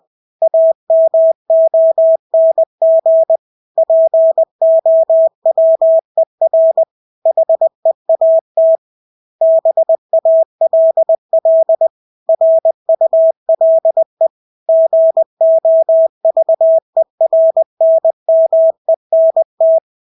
Morse code, heard as a keyed tone, about 20 wpm, 650 Hz. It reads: AMONG POWER HEAT BALL RULE GOVERNMENT
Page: Engelske ord 20 wpm